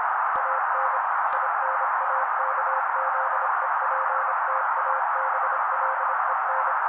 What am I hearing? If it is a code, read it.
Morse code, heard as a keyed tone, about 26 wpm, 530 Hz. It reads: AN INAKZEPTABLEN
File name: daily / 25wpm noise